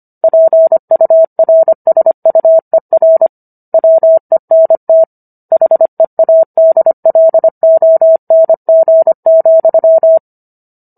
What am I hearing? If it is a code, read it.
Morse code, heard as a keyed tone, about 25 wpm, 650 Hz. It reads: PURSUER WENT HEADLONG,